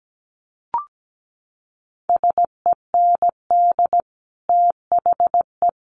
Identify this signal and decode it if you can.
Morse code, heard as a keyed tone, about 17 wpm, 700 Hz. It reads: SEND THE
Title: Sentences from Top 1500 Words 17wpm